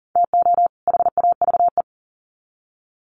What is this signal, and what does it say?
Morse code, heard as a keyed tone, about 40 wpm, 700 Hz. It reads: TO HAVE
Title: Sets of 2 Words 40wpm